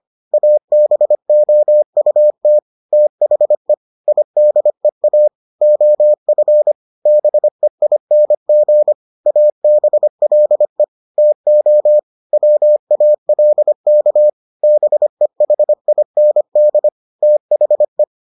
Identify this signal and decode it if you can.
Morse code, heard as a keyed tone, about 25 wpm, 600 Hz. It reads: ABOUT THE IDEA OF BEING ABLE TO WALK BEHIND THE